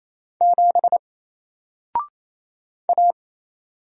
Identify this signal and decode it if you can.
Morse code, heard as a keyed tone, about 28 wpm, 700 Hz. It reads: 7 A
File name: Single-Letter-Number-28wpm